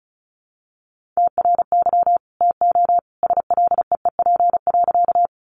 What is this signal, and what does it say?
Morse code, heard as a keyed tone, about 34 wpm, 700 Hz. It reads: TRY TO SLEEP.
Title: Sentences from Top 500 Words 35wpm